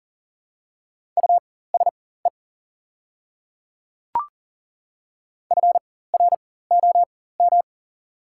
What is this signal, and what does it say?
Morse code, heard as a keyed tone, about 40 wpm, spaced as for 19 wpm, 690 Hz. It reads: USE FROM
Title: 4x Character Spacing - Top 200 Words - 40wpm